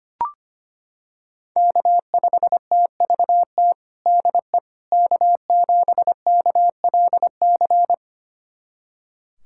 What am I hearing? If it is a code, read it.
Morse code, heard as a keyed tone, about 25 wpm, 700 Hz. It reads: K5TVT DE K7KLC
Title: USA Callsign DE Callsign - 5 Characters - 25wpm